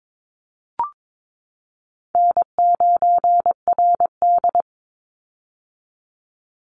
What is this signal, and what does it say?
Morse code, heard as a keyed tone, about 22 wpm, 700 Hz. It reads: N9RD
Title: Easy Call Signs - 22wpm